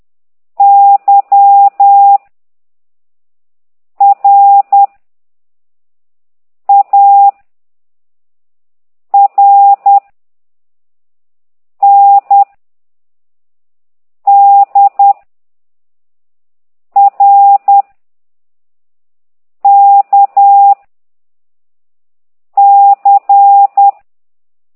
Morse code, heard as a keyed tone, about 10 wpm, 800 Hz. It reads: Y R A R N D R K C